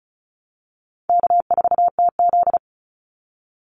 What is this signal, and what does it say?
Morse code, heard as a keyed tone, about 34 wpm, 700 Hz. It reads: K4TZ
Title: Call Signs - 35wpm